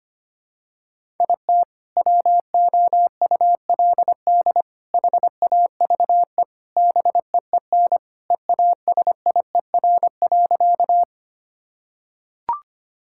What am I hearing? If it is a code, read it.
Morse code, heard as a keyed tone, about 25 wpm, 700 Hz. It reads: IT WOULD HAVE BEEN EASIER.
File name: practice / 25wpm / Sentences with 4-Word Phrases - 25wpm